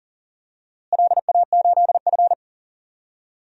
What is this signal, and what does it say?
Morse code, heard as a keyed tone, about 37 wpm, 700 Hz. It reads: LA8F
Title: DXCC-100 Call Signs - 4 Characters - Repeated 2x - 40wpm